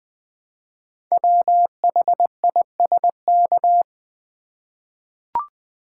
Morse code, heard as a keyed tone, about 20 wpm, 700 Hz. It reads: WHISK
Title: Easy Words with Hints - 20wpm